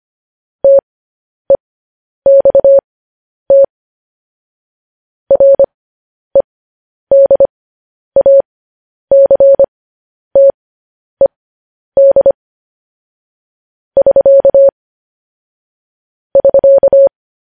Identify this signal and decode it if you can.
Morse code, heard as a keyed tone, about 25 wpm, spaced as for 10 wpm, 550 Hz. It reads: TEXT REDACTED <SK> <SK>